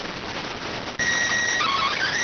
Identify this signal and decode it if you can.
SSTV (slow-scan television), PD 290.